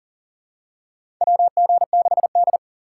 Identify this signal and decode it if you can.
Morse code, heard as a keyed tone, about 40 wpm, 700 Hz. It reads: WG6D
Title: USA Call Signs - 4 Characters - 4x Character Spacing - 40wpm